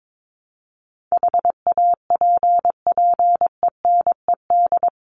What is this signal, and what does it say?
Morse code, heard as a keyed tone, about 22 wpm, 700 Hz. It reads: HAPPENED
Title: ED Suffix - Top 100 Words with Base Words 22wpm